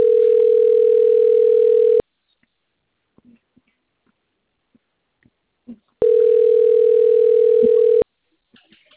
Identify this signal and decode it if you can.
Ringback tone (North America), 440 + 480 Hz.
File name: rg-1600-unknown- - - .20071